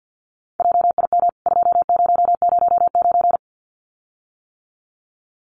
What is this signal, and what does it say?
Morse code, heard as a keyed tone, about 50 wpm, 700 Hz. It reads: JIM 2009
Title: CWT Exchange 50wpm